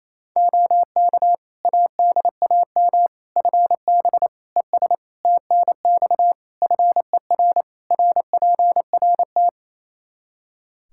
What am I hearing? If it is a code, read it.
Morse code, heard as a keyed tone, about 27 wpm, 700 Hz. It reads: OK ADAM FB ES TNX FER RPRT
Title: Practice QSO - Exchange 2 - Info - 28wpm